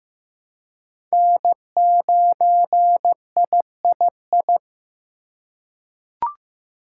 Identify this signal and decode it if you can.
Morse code, heard as a keyed tone, about 15 wpm, 700 Hz. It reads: N9III
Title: USA Call Signs - 5 Characters - 3x Character Spacing - 15wpm